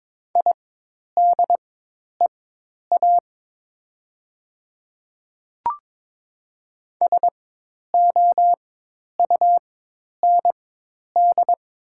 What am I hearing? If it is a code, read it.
Morse code, heard as a keyed tone, about 22 wpm, spaced as for 10 wpm, 700 Hz. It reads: IDEA SOUND
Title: 4x Character Spacing - Top 300 Words - 22wpm